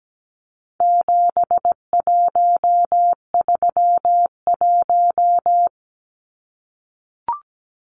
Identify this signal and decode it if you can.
Morse code, heard as a keyed tone, about 17 wpm, 700 Hz. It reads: 7131
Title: Numbers - 4-Digit 17wpm